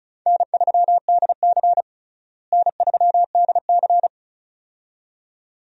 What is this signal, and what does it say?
Morse code, heard as a keyed tone, about 36 wpm, 700 Hz. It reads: N3DC N3DC
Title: USA Call Signs - 4 Characters - Repeated 2x - 35wpm